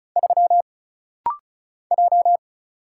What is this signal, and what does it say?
Morse code, heard as a keyed tone, about 36 wpm, 700 Hz. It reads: 3 J
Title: Single Letter-Number Alt - Rapid-Fire 35wpm